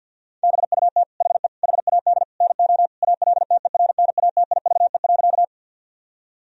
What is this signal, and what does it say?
Morse code, heard as a keyed tone, about 50 wpm, 700 Hz. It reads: BUT HE HAD NO ALTERNATIVE.
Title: Sentences from Top 1000 Words - Review Part 4 - 50wpm